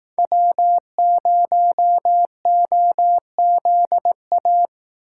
Morse code, heard as a keyed tone, about 18 wpm, 700 Hz. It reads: W0OZA
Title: Speed Racing - USA 1x3 Call Signs 15wpm